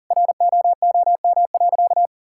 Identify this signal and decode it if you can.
Morse code, heard as a keyed tone, about 40 wpm, 700 Hz. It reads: ROOM.
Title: Sentences from Top 1000 Words - Review Part 6 - 40wpm